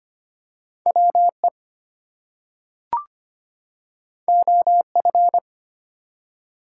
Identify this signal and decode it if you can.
Morse code, heard as a keyed tone, about 25 wpm, 700 Hz. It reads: WE OF
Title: Two Letter Words - 25wpm